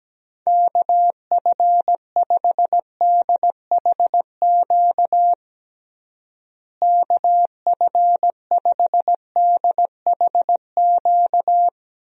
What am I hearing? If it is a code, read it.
Morse code, heard as a keyed tone, about 17 wpm, 700 Hz. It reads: KF5DHQ KF5DHQ
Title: USA Call Signs - 6 Characters - Repeated 3x - 17wpm